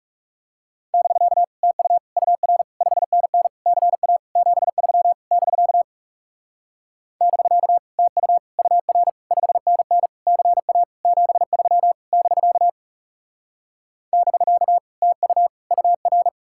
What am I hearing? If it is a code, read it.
Morse code, heard as a keyed tone, about 36 wpm, 700 Hz. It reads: <BK> TU UR 5NN CA 73 <BK> <BK> TU UR 5NN CA 73 <BK> <BK> TU UR
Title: Speed-Racing Common QSO Phrases - 30wpm